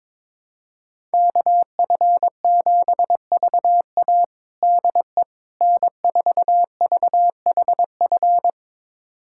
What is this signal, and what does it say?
Morse code, heard as a keyed tone, about 22 wpm, 700 Hz. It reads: KF7VA DE N4VHF
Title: USA Callsign DE Callsign - 5 Characters - 22wpm